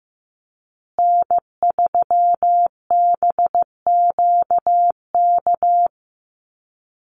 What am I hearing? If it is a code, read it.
Morse code, heard as a keyed tone, about 15 wpm, 700 Hz. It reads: N3BQK
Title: USA Call Signs Repeated 2x 15wpm